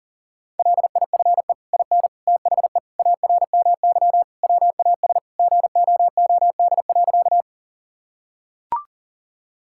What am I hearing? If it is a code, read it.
Morse code, heard as a keyed tone, about 40 wpm, 700 Hz. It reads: LIFE IN THE ARMY WAS GOOD.
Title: Sentences from Top 1000 Words - Review Part 5 - 40wpm